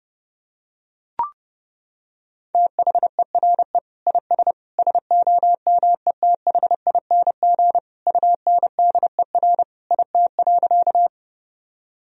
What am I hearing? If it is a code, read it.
Morse code, heard as a keyed tone, about 30 wpm, 700 Hz. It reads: THERE IS SOMETHING UNDER IT.
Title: Sentences from Top 1000 Words - Review Part 1 - 30wpm